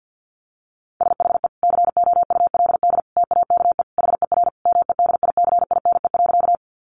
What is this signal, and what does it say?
Morse code, heard as a keyed tone, about 49 wpm, 700 Hz. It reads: SHE COULD TAKE HER MEDICINE.
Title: Sentences from Top 900 Words 50wpm